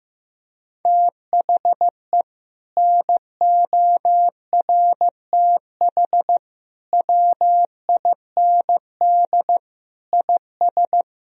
Morse code, heard as a keyed tone, about 15 wpm, 700 Hz. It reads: THE NORTH WIND IS
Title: Sentences from Top 1000 Words - Review Part 2 - 15wpm